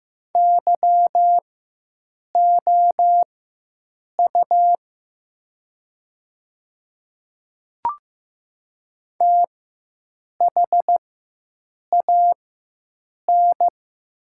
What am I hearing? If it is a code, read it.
Morse code, heard as a keyed tone, about 15 wpm, spaced as for 7 wpm, 700 Hz. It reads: YOU THAN
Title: 4x Character Spacing - Top 100 Words - 15wpm